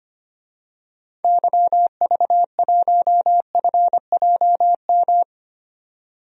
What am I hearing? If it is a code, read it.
Morse code, heard as a keyed tone, about 25 wpm, 700 Hz. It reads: YV1FJM
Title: DXCC-100 Call Signs - 6 Characters - 25wpm